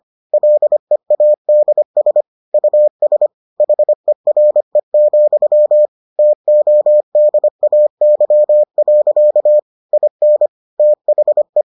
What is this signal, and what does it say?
Morse code, heard as a keyed tone, about 25 wpm, 600 Hz. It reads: LEADS US HERE, TODAY. IN THE